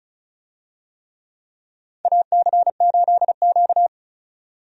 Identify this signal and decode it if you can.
Morse code, heard as a keyed tone, about 34 wpm, 700 Hz. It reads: AC8Q
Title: USA Call Signs - 4 Characters - 35wpm